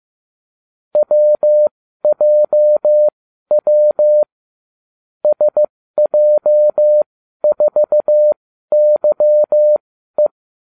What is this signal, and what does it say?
Morse code, heard as a keyed tone, about 15 wpm, spaced as for 12 wpm, 600 Hz. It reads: WJW SJ4YE